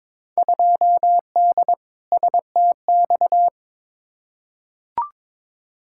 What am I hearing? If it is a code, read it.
Morse code, heard as a keyed tone, about 22 wpm, 700 Hz. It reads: 2D STX
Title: Field Day Exchanges - 22wpm